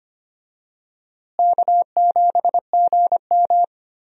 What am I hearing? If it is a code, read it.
Morse code, heard as a keyed tone, about 25 wpm, 700 Hz. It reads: K7GM
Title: USA Call Signs - 4 Characters - Repeated 2x - 25wpm